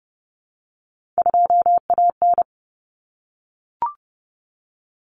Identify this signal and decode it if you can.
Morse code, heard as a keyed tone, about 30 wpm, 700 Hz. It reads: 2AN
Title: Common Cut Numbers - 3-Digit 30wpm